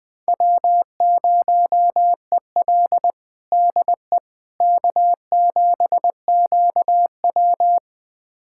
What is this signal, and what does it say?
Morse code, heard as a keyed tone, about 20 wpm, 700 Hz. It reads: W0EL DE K7QW
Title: USA Callsign DE Callsign - 4 Characters - 20wpm